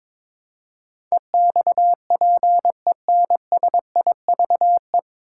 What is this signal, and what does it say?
Morse code, heard as a keyed tone, about 22 wpm, 700 Hz. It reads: EXPENSIVE
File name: 9 Letter Words - First 3 Letters Given - 22wpm